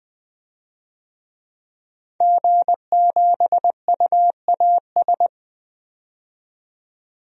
Morse code, heard as a keyed tone, about 20 wpm, 700 Hz. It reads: G7UAS
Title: DXCC-100 Call Signs - 5 Characters - 20wpm